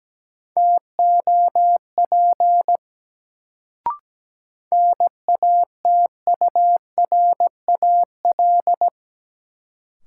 Morse code, heard as a keyed tone, about 17 wpm, 700 Hz. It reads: TOP NATURAL
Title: Sentences from Top 700 Words - New Word Drill - 17wpm